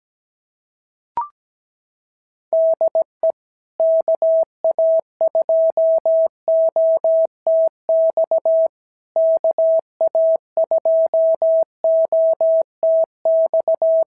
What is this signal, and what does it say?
Morse code, heard as a keyed tone, about 18 wpm, 650 Hz. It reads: DE KA2OTX KA2OTX
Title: SOTA QSO Practice 17wpm